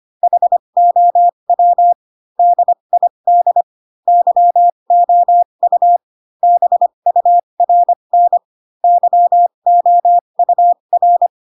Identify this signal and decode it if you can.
Morse code, heard as a keyed tone, about 25 wpm, spaced as for 22 wpm, 700 Hz. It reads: HOW DID YOU BURN YOUR
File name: Pinocchio / Pinocchio ch7 18 25 ik1hge